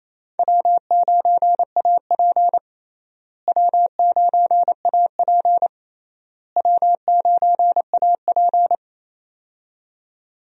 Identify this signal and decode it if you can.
Morse code, heard as a keyed tone, about 27 wpm, 700 Hz. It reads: W9AP W9AP W9AP
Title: USA Call Signs - 4 Characters - Repeated 3x - 28wpm